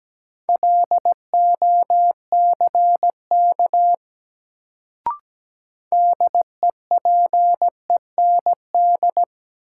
Morse code, heard as a keyed tone, about 17 wpm, 700 Hz. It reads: LOCK DEPEND
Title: Sentences from Top 1300 Words - New Word Drill - 17wpm